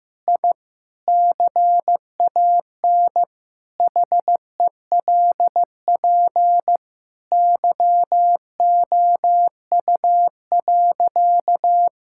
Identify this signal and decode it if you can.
Morse code, heard as a keyed tone, about 15 wpm, 700 Hz. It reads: I CAN HELP YOU.
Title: Sentences from Top 1000 Words - Review Part 6 - 15wpm